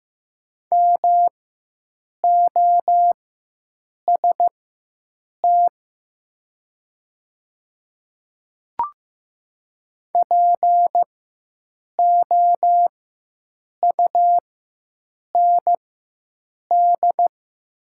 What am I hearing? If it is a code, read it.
Morse code, heard as a keyed tone, about 15 wpm, spaced as for 7 wpm, 700 Hz. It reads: MOST POUND
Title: 4x Character Spacing - Top 500 Words - 15wpm